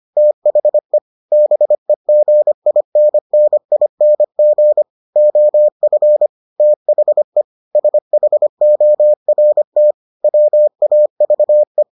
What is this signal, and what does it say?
Morse code, heard as a keyed tone, about 25 wpm, 600 Hz. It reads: THE BEGINNING OF THE SHORT WAVE